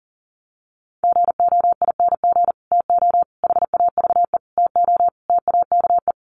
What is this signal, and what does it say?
Morse code, heard as a keyed tone, about 40 wpm, 700 Hz. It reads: GOING TO HAVE TO TAKE
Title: Sets of 5 Words Encore 40wpm